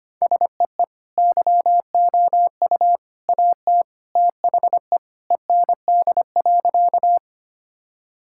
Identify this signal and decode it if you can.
Morse code, heard as a keyed tone, about 25 wpm, 700 Hz. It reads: SEE YOU AT THE END.
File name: practice / 25wpm / Sentences with 3-Word Phrases - 25wpm